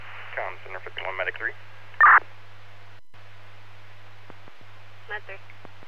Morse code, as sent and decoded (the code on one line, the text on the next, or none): none